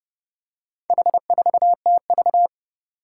Morse code, heard as a keyed tone, about 30 wpm, 700 Hz. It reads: H4TV
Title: Four Challenging Letter-Number Combinations 30wpm